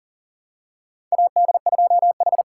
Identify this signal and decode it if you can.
Morse code, heard as a keyed tone, about 40 wpm, 690 Hz. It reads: AD2H
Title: USA Call Signs - 4 Characters - 2x Character Spacing - 40wpm